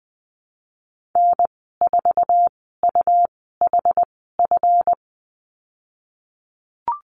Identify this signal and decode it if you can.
Morse code, heard as a keyed tone, about 20 wpm, spaced as for 15 wpm, 700 Hz. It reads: N4UHF
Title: 140 Lesson 23g - Letter f - Call Signs 20wpm